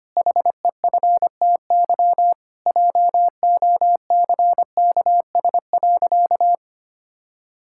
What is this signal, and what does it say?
Morse code, heard as a keyed tone, about 25 wpm, 700 Hz. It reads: HEFTY JOCKS.
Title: Pangram Challenge v2 - 25wpm